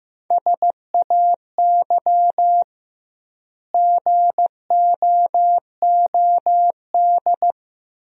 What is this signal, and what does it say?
Morse code, heard as a keyed tone, about 15 wpm, 700 Hz. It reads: SAY GOOD
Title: 2x Word-Spacing - Sentences from Top 300 Words - 15wpm